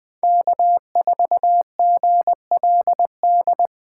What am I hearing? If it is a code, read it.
Morse code, heard as a keyed tone, about 24 wpm, 700 Hz. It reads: K4GLD
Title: Speed Racing - USA 1x3 Call Signs 20wpm